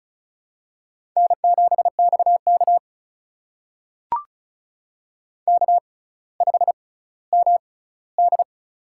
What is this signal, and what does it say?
Morse code, heard as a keyed tone, about 36 wpm, spaced as for 18 wpm, 700 Hz. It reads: N7XK K5MD
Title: USA Call Signs - 4 Characters - 6x Character Spacing - 35wpm